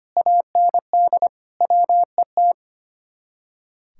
Morse code, heard as a keyed tone, about 30 wpm, 700 Hz. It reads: AND WET
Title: Speed-Racing Common QSO Phrases Encore - 25wpm